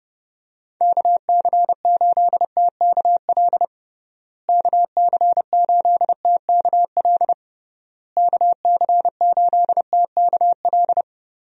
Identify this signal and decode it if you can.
Morse code, heard as a keyed tone, about 30 wpm, 700 Hz. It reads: KC8TKL KC8TKL KC8TKL
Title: USA Call Signs - 6 Characters - Repeated 3x - 30wpm